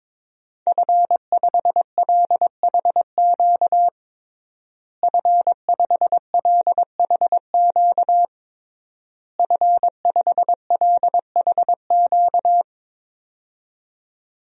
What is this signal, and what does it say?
Morse code, heard as a keyed tone, about 22 wpm, 700 Hz. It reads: F5LHQ F5LHQ F5LHQ
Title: DXCC-100 Call Signs - 5 Characters - Repeated 3x - 22wpm